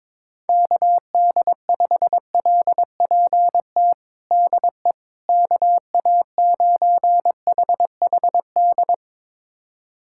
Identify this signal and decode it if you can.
Morse code, heard as a keyed tone, about 22 wpm, 700 Hz. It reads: KD5LPT DE KA9HHD
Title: USA Callsign DE Callsign 22wpm